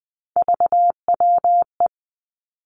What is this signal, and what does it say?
Morse code, heard as a keyed tone, about 20 wpm, 700 Hz. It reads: VWE
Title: Three Challenging Letter-Number Combinations 20wpm